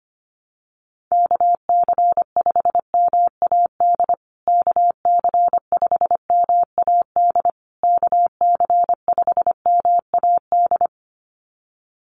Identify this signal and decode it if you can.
Morse code, heard as a keyed tone, about 25 wpm, 700 Hz. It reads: KC5MAD KC5MAD KC5MAD
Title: USA Call Signs Repeated 3x 25wpm